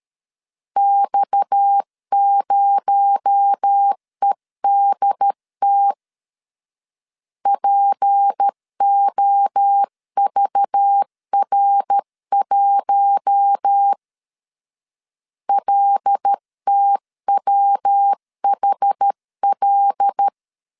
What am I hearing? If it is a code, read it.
Morse code, heard as a keyed tone, about 13 wpm, 800 Hz. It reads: X0EDT POVR1 LTWHL
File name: CW / DET / 1 6 1500C S C 12wpm 3533